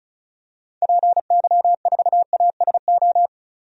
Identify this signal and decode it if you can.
Morse code, heard as a keyed tone, about 36 wpm, 700 Hz. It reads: PY4ASO
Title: DXCC-100 Call Signs - 6 Characters - 35wpm